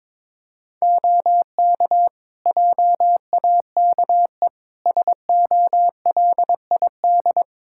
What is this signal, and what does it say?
Morse code, heard as a keyed tone, about 22 wpm, 700 Hz. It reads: OK JAKE SOLID
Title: Practice QSO - Exchange 3 - More Info - 22wpm